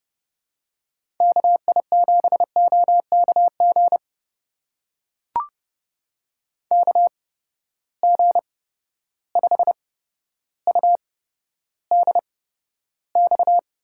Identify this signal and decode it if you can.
Morse code, heard as a keyed tone, about 30 wpm, spaced as for 13 wpm, 700 Hz. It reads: KI7OKG KG5UDX
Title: USA Call Signs - 6 Characters - 8x Character Spacing - 30wpm